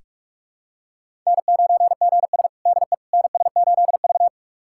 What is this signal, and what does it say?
Morse code, heard as a keyed tone, about 47 wpm, 700 Hz. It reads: N9GS DE NS8V
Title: USA Callsign DE Callsign - 4 Characters - 45wpm